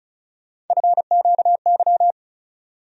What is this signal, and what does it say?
Morse code, heard as a keyed tone, about 34 wpm, 700 Hz. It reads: FQY
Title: USA Call Sign Postfixes - 3 Letters 35wpm